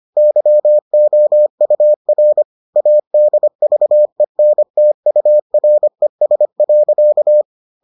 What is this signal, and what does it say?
Morse code, heard as a keyed tone, about 25 wpm, 600 Hz. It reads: YOUR ADVENTURES.